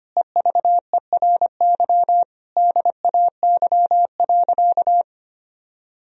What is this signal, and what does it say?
Morse code, heard as a keyed tone, about 25 wpm, 700 Hz. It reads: EVERY DAY.
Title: Sentences from Top 1000 Words - Review Part 1 - 25wpm